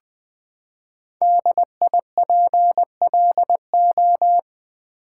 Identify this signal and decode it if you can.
Morse code, heard as a keyed tone, about 20 wpm, 700 Hz. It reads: DIPLO
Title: Build A Word - 5 Letters At A Time - 10-Letter Words - 20wpm